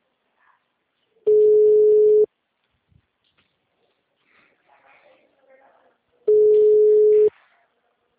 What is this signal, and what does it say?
Ringback tone (Europe, 425 Hz), with speech.